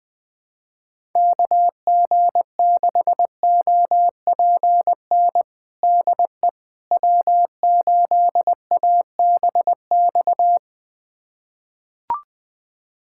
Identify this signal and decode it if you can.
Morse code, heard as a keyed tone, about 21 wpm, 700 Hz. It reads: KG6OPN DE W8ABX
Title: USA Callsign DE Callsign 20wpm